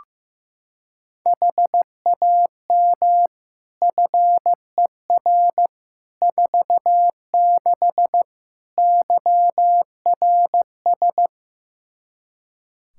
Morse code, heard as a keyed tone, about 15 wpm, 700 Hz. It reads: HAM FER 46 YRS
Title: Practice QSO - Exchange 3 - More Info - 15wpm